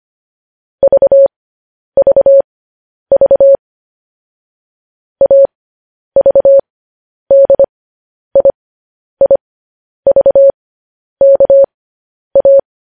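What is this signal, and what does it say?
Morse code, heard as a keyed tone, about 25 wpm, spaced as for 10 wpm, 550 Hz. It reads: VVV AVDIIVKA
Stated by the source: News headlines sent in Morse at 10 wpm.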